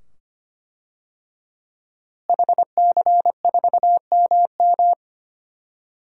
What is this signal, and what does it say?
Morse code, heard as a keyed tone, about 25 wpm, 700 Hz. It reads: HC4MM
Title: DXCC-100 Call Signs - 5 Characters - Repeated 2x - 25wpm